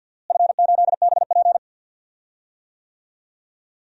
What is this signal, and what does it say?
Morse code, heard as a keyed tone, about 52 wpm, 700 Hz. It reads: F8BP
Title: DXCC-100 Call Signs - 4 Characters - Repeated 2x - 50wpm